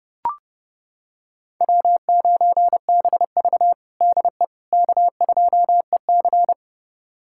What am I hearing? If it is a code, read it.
Morse code, heard as a keyed tone, about 32 wpm, 700 Hz. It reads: W9BV DE K2EC
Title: USA Callsign DE Callsign - 4 Characters - 30wpm